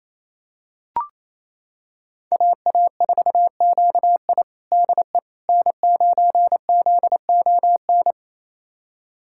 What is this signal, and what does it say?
Morse code, heard as a keyed tone, about 28 wpm, 700 Hz. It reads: AA4QI DE N9ZON
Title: USA Callsign DE Callsign - 5 Characters - 28wpm